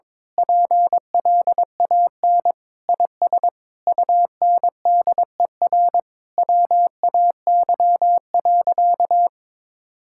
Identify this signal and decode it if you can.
Morse code, heard as a keyed tone, about 22 wpm, 700 Hz. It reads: PLAN IS UNDER WAY.
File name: Sentences with 9-Word Phrases - 22wpm